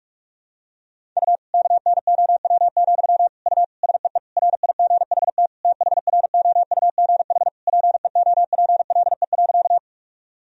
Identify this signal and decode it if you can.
Morse code, heard as a keyed tone, about 46 wpm, 700 Hz. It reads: U KNOW, U SEE RIGHT THROUGH PEOPLE.
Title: Sentences from Top 200 Words and Abbreviations 45wpm